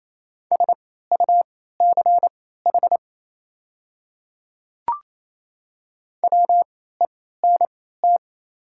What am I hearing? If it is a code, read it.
Morse code, heard as a keyed tone, about 27 wpm, spaced as for 16 wpm, 700 Hz. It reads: SUCH WENT
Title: 3x Character Spacing - Top 200 Words - 28wpm